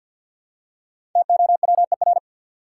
Morse code, heard as a keyed tone, about 48 wpm, 700 Hz. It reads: TOWER
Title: Five Letter Words - 50wpm